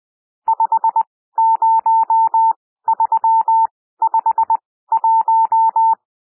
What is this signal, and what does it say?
Morse code, heard as a keyed tone, about 20 wpm, spaced as for 14 wpm, 900 Hz. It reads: 50351
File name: cwmp3 / N1 SOUND NUMBERS 20 wpm